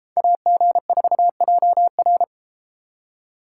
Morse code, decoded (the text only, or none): AG4JR